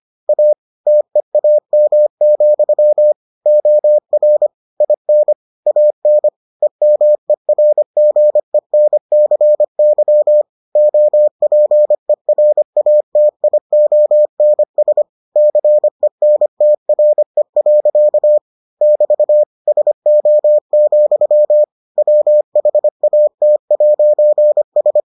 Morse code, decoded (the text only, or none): A TEAM, OR IN AN EMERGENCY OPERATIONS CENTRE. <BT> SO, WHAT'S